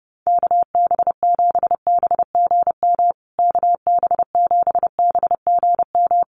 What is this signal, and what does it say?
Morse code, heard as a keyed tone, about 28 wpm, 700 Hz. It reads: KB7BGM KB7BGM
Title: USA Call Signs Repeated 3x 30wpm